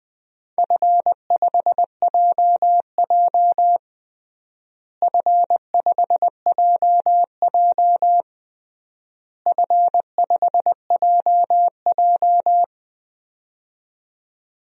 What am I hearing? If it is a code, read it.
Morse code, heard as a keyed tone, about 20 wpm, 700 Hz. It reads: F5JJ F5JJ F5JJ
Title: DXCC-100 Call Signs - 4 Characters - Repeated 3x - 20wpm